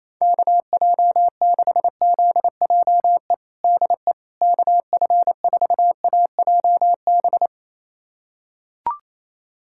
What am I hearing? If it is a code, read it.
Morse code, heard as a keyed tone, about 28 wpm, 700 Hz. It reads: KJ6ZJE DE KF4AJB
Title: USA Callsign DE Callsign 28wpm